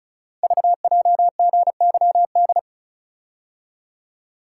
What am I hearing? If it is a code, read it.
Morse code, heard as a keyed tone, about 36 wpm, 700 Hz. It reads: VJGYD
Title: Code Groups - 5 Letters - 35wpm